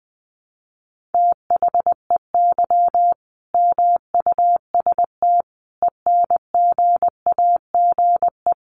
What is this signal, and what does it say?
Morse code, heard as a keyed tone, about 20 wpm, 700 Hz. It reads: THEY MUST ENGAGE
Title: Sentences from Top 1500 Words 20wpm